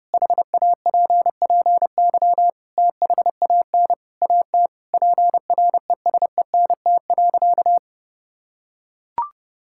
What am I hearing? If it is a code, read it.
Morse code, heard as a keyed tone, about 32 wpm, 700 Hz. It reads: HAPPY THAN AT PRESENT.
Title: Sentences from Top 1000 Words - Review Part 5 - 30wpm